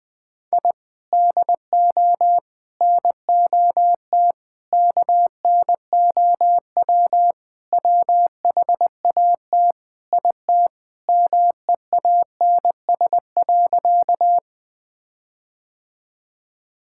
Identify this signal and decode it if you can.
Morse code, heard as a keyed tone, about 20 wpm, 700 Hz. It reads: I DO NOT KNOW WHAT IT MEANS.
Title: Sentences with 5-Word Phrases - 20wpm